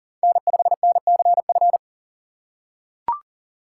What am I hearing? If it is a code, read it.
Morse code, heard as a keyed tone, about 40 wpm, 700 Hz. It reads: N5NCF
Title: USA Call Signs - 5 Characters - 4x Character Spacing - 40wpm